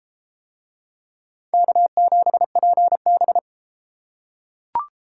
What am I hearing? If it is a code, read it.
Morse code, heard as a keyed tone, about 31 wpm, 700 Hz. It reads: K7PB